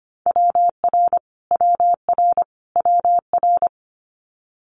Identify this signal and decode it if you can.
Morse code, heard as a keyed tone, about 25 wpm, 700 Hz. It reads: WR WR WR
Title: Multi-letter Phonemes Repeated 3x 25wpm